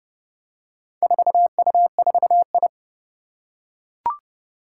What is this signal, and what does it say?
Morse code, heard as a keyed tone, about 31 wpm, 700 Hz. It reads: 4U4I
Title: Four Challenging Letter-Number Combinations 30wpm